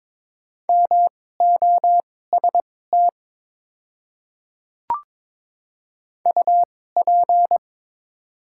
Morse code, heard as a keyed tone, about 22 wpm, spaced as for 16 wpm, 700 Hz. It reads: MOST UP
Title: 2x Character Spacing - Top 100 Words - 22wpm